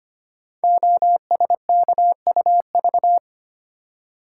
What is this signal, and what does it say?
Morse code, heard as a keyed tone, about 25 wpm, 700 Hz. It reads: OSKUV
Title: Code Groups - 5 Letters - 25wpm